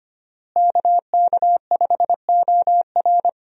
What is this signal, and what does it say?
Morse code, heard as a keyed tone, about 25 wpm, 700 Hz. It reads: KK5OR
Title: USA Call Signs - 5 Characters - Repeated 3x - 25wpm